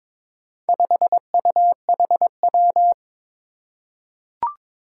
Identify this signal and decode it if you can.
Morse code, heard as a keyed tone, about 23 wpm, 700 Hz. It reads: 5UHW
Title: Four Challenging Letter-Number Combinations 22wpm